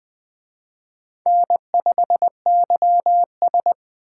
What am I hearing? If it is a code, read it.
Morse code, heard as a keyed tone, about 21 wpm, 700 Hz. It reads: N5YS